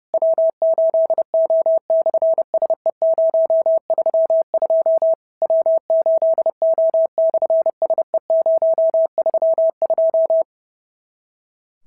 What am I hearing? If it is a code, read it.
Morse code, heard as a keyed tone, about 30 wpm, 650 Hz. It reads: W8O/SE032 W8O/SE032
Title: SOTA QSO Practice 30wpm